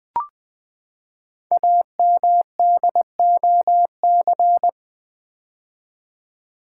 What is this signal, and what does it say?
Morse code, heard as a keyed tone, about 20 wpm, 700 Hz. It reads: AMDOC
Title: Code Groups - 5 Letters - 20wpm